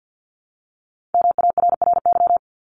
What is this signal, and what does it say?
Morse code, heard as a keyed tone, about 50 wpm, 700 Hz. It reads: MARRY
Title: ED Suffix - Top 100 Words with Base Words 50wpm